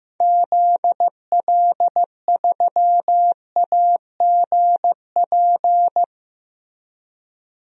Morse code, heard as a keyed tone, about 15 wpm, 700 Hz. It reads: ZL3AGP
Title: DXCC-100 Call Signs - 6 Characters - Repeated 2x - 15wpm